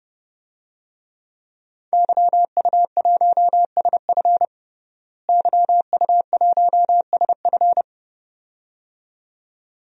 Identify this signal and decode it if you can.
Morse code, heard as a keyed tone, about 30 wpm, 700 Hz. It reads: YU1SF YU1SF
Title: DXCC-100 Call Signs - 5 Characters - Repeated 2x - 30wpm